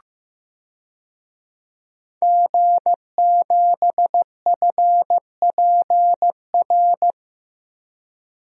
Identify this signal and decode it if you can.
Morse code, heard as a keyed tone, about 15 wpm, 700 Hz. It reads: G7FPR
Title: DXCC-100 Call Signs - 5 Characters - Repeated 2x - 15wpm